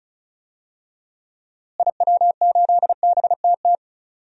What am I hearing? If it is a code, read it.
Morse code, heard as a keyed tone, about 36 wpm, 700 Hz. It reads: IW8BTT
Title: DXCC-100 Call Signs - 6 Characters - Repeated 3x - 35wpm